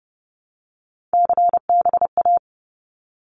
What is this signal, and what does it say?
Morse code, heard as a keyed tone, about 30 wpm, 700 Hz. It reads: CBA
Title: Abbreviations 30wpm